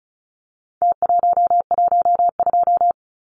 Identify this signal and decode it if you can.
Morse code, heard as a keyed tone, about 34 wpm, 700 Hz. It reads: T112
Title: Numbers - Contest Style - 4-Digit 35wpm